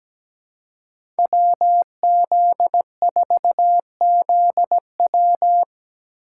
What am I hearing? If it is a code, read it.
Morse code, heard as a keyed tone, about 17 wpm, 700 Hz. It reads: WZ4ZW
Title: USA Call Signs - 5 Characters - 2x Character Spacing - 17wpm